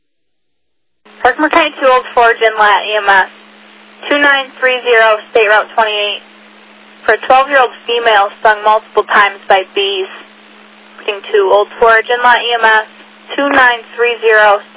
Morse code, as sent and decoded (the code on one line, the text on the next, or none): none